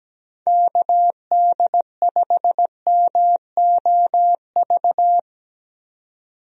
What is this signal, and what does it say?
Morse code, heard as a keyed tone, about 17 wpm, 700 Hz. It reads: KD5MOV
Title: USA Call Signs - 6 Characters - 17wpm